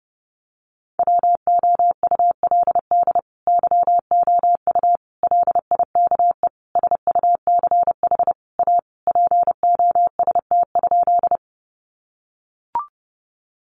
Morse code, heard as a keyed tone, about 30 wpm, 700 Hz. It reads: WOULD YOU LIKE SUCH A POST?
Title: Sentences from Top 1000 Words 30wpm